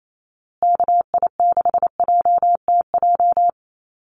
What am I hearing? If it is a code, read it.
Morse code, heard as a keyed tone, about 28 wpm, 700 Hz. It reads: KI6JTJ
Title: USA Call Signs Repeated 2x 28wpm